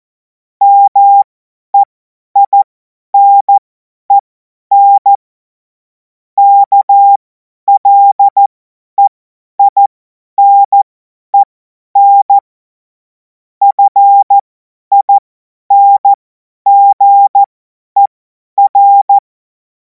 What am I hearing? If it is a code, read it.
Morse code, heard as a keyed tone, about 14 wpm, spaced as for 10 wpm, 800 Hz. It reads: MEINEN KLEINEN FINGER
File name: Morsekurs / Witze / Witz 09. Juni 2026 50 F70